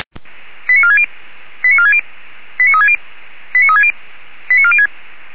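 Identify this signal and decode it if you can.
Five-tone selective call (ZVEI), heard as quick runs of five tones, tones of about 70 ms. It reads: 89460 89460 89360 89360 89406
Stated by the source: ZVEI five-tone selective call 89466 received via SDR.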